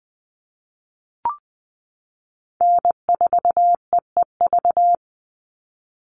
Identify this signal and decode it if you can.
Morse code, heard as a keyed tone, about 20 wpm, 700 Hz. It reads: N4EEV
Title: Call Signs Repeated 3x - 20wpm